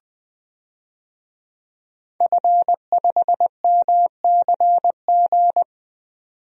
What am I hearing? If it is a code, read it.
Morse code, heard as a keyed tone, about 21 wpm, 700 Hz. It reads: F5MCG
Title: DXCC-100 Call Signs - 5 Characters - 20wpm